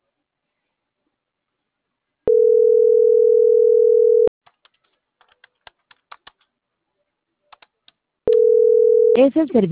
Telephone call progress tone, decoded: ringback tone

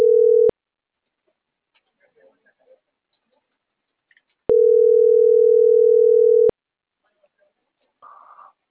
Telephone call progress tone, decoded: ringback tone

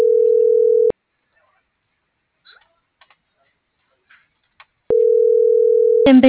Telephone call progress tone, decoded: ringback tone

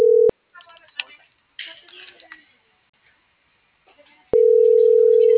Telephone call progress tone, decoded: ringback tone

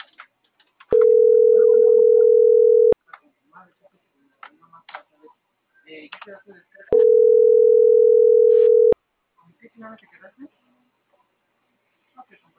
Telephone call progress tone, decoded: ringback tone

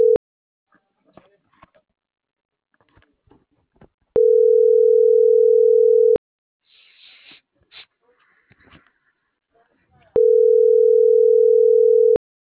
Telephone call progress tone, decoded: ringback tone

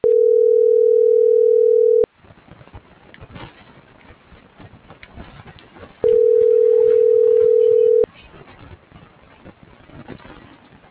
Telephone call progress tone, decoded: ringback tone

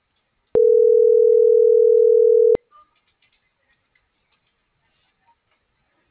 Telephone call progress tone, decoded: ringback tone

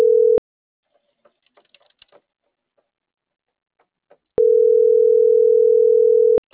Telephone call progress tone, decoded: ringback tone

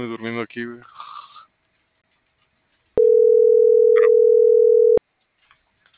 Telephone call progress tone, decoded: ringback tone

